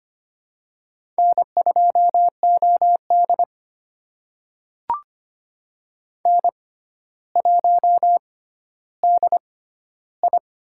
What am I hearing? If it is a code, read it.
Morse code, heard as a keyed tone, about 25 wpm, spaced as for 13 wpm, 700 Hz. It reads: N2OD N1DI
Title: USA Call Signs - 4 Characters - 6x Character Spacing - 25wpm